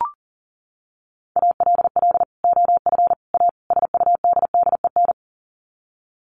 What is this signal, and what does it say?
Morse code, heard as a keyed tone, about 40 wpm, 700 Hz. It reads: ALL OF A SUDDEN
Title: Sets of 4 Words 40wpm